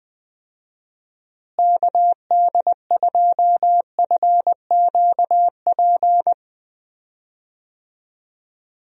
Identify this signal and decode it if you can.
Morse code, heard as a keyed tone, about 20 wpm, 700 Hz. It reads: KD2FQP
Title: USA Call Signs - 6 Characters - Repeated 3x - 20wpm